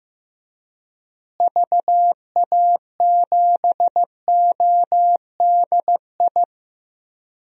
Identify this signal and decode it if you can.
Morse code, heard as a keyed tone, about 15 wpm, 700 Hz. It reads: VA7ODI
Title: DXCC-100 Call Signs - 6 Characters - Repeated 2x - 15wpm